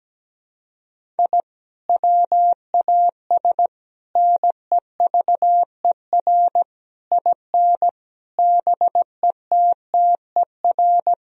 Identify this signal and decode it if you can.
Morse code, heard as a keyed tone, about 17 wpm, 700 Hz. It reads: I WAS NEVER IN BETTER
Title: Sentences from Top 1000 Words - Review Part 3 - 17wpm